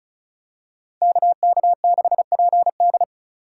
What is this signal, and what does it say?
Morse code, heard as a keyed tone, about 36 wpm, 700 Hz. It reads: KK6PD
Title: USA Call Signs - 5 Characters - 2x Character Spacing - 35wpm